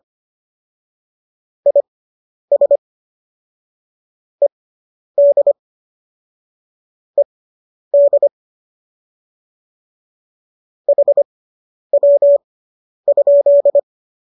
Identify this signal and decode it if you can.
Morse code, heard as a keyed tone, about 25 wpm, spaced as for 10 wpm, 570 Hz. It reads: IS ED ED HW?